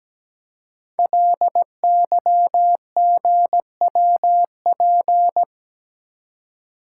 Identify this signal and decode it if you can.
Morse code, heard as a keyed tone, about 17 wpm, 700 Hz. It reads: LYGWP
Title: Code Groups - 5 Letter:Numbers - 17wpm